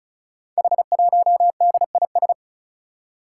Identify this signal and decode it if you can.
Morse code, heard as a keyed tone, about 36 wpm, 700 Hz. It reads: H1DIS
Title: Five Challenging Letter-Number Combinations 35wpm